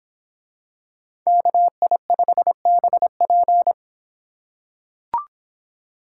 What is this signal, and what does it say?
Morse code, heard as a keyed tone, about 24 wpm, 700 Hz. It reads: KI5BP